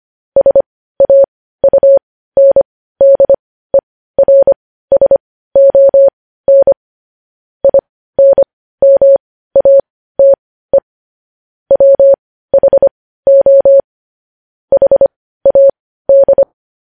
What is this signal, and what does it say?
Morse code, heard as a keyed tone, about 25 wpm, spaced as for 15 wpm, 550 Hz. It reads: SAUNDERSON INMATE WHO HAD
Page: news-headlines-in-morse-code-at-15-wpm